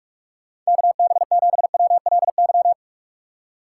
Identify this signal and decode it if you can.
Morse code, heard as a keyed tone, about 44 wpm, 700 Hz. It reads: KB7WLY